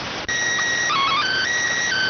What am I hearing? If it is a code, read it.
SSTV (slow-scan television), PD 290.